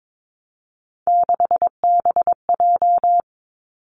Morse code, heard as a keyed tone, about 22 wpm, 700 Hz. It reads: 6BJ
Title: Three Challenging Letter-Number Combinations 22wpm